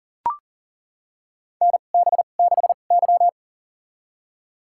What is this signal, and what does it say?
Morse code, heard as a keyed tone, about 38 wpm, spaced as for 28 wpm, 700 Hz. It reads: NB6Y
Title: USA Call Signs - 4 Characters - 2x Character Spacing - 40wpm